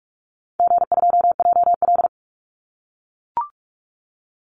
Morse code, heard as a keyed tone, about 44 wpm, 700 Hz. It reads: G2JL
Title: Easy Call Signs - 45wpm